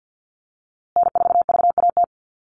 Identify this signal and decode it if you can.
Morse code, heard as a keyed tone, about 50 wpm, 700 Hz. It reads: N4VAT
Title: USA Call Signs Repeated 3x 50wpm